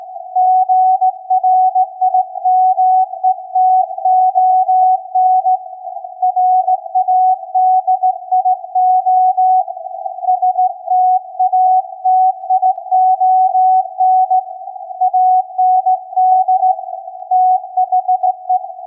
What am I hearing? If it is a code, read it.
Morse code, heard as a keyed tone, about 17 wpm, 740 Hz. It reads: GRIMETON RADIO STATION AND THE
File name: iz7slz / VLF / SAQ 16-11-2022